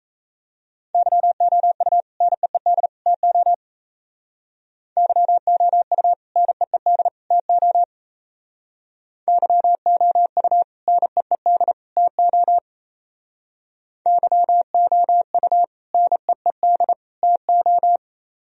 Morse code, heard as a keyed tone, about 35 wpm, 700 Hz. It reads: YOU NEED TO YOU NEED TO YOU NEED TO YOU NEED TO
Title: Speed-Racing - Sets of 3 Words 28wpm